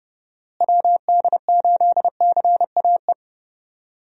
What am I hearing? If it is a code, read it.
Morse code, heard as a keyed tone, about 30 wpm, 700 Hz. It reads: WD8CAE
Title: USA Call Signs - 6 Characters - Repeated 3x - 30wpm